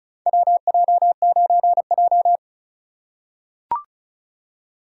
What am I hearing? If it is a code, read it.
Morse code, heard as a keyed tone, about 36 wpm, 700 Hz. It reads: WJ9J
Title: USA Call Signs - 4 Characters - 6x Character Spacing - 35wpm